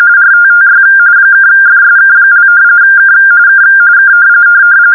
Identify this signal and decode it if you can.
RTTY, about 45.45 baud. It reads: RTTY (FSK) K K K